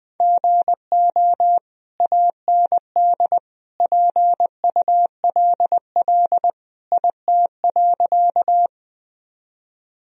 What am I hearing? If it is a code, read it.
Morse code, heard as a keyed tone, about 20 wpm, 700 Hz. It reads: GO AND PULL IT.
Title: Sentences from Top 1000 Words - Review Part 2 - 20wpm